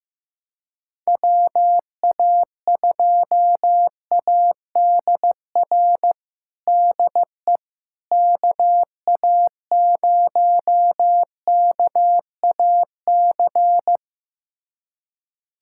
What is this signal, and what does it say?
Morse code, heard as a keyed tone, about 15 wpm, 700 Hz. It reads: WA2ADR DE KA0KAC
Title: USA Callsign DE Callsign - 6 Characters - 15wpm